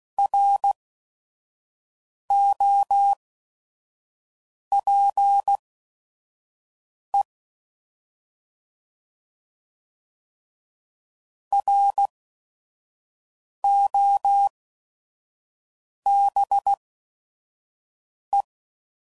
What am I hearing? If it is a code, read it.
Morse code, heard as a keyed tone, about 16 wpm, spaced as for 5 wpm, 800 Hz. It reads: ROPE ROBE
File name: Prefixes / By First Letter / Rs